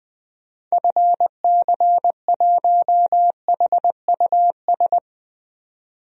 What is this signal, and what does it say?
Morse code, heard as a keyed tone, about 21 wpm, 700 Hz. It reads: FC1HUS
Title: DXCC-100 Call Signs - 6 Characters - 20wpm